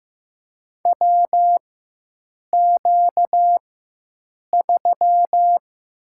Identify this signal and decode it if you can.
Morse code, heard as a keyed tone, about 15 wpm, 700 Hz. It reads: W Q 3
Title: USA Call Signs - 4 Characters - 4x Character Spacing - 15wpm